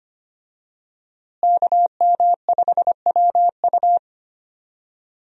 Morse code, heard as a keyed tone, about 25 wpm, 700 Hz. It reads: KM5WU
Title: USA Call Signs - 5 Characters - Repeated 2x - 25wpm